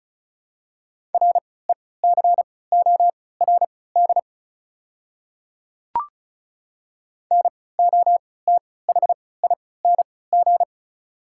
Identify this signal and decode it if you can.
Morse code, heard as a keyed tone, about 36 wpm, spaced as for 20 wpm, 690 Hz. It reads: RECORD NOTHING
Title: 3x Character Spacing - Top 500 Words - 35wpm